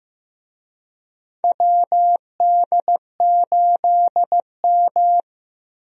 Morse code, heard as a keyed tone, about 15 wpm, 700 Hz. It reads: WD8M
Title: USA Call Signs - 4 Characters - Repeated 3x - 15wpm